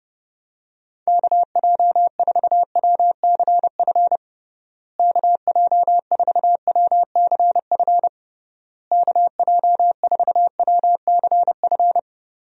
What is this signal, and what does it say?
Morse code, heard as a keyed tone, about 29 wpm, 700 Hz. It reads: KJ4WCF KJ4WCF KJ4WCF
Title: USA Call Signs - 6 Characters - Repeated 3x - 30wpm